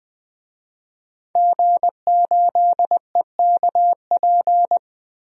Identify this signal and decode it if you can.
Morse code, heard as a keyed tone, about 21 wpm, 700 Hz. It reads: G8EKP
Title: DXCC-100 Call Signs - 5 Characters - Repeated 3x - 20wpm